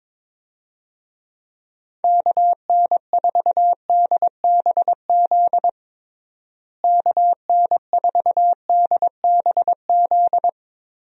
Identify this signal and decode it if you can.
Morse code, heard as a keyed tone, about 22 wpm, 700 Hz. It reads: KN4DBZ KN4DBZ
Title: USA Call Signs - 6 Characters - Repeated 2x - 22wpm